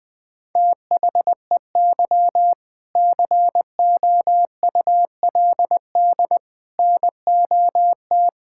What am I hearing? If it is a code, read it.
Morse code, heard as a keyed tone, about 20 wpm, 700 Hz. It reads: THEY COULD NOT
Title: Sentences from Top 1000 Words - Review Part 6 - 20wpm